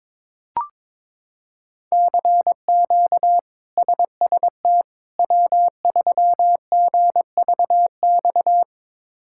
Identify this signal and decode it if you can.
Morse code, heard as a keyed tone, about 22 wpm, 700 Hz. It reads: CQ SST W3GVX
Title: K1USN SST QSO Practice - 22wpm